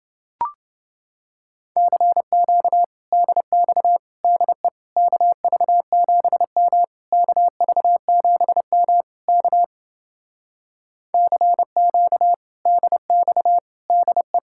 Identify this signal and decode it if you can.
Morse code, heard as a keyed tone, about 29 wpm, 700 Hz. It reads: CQ DX DE KV7M KV7M K CQ DX DE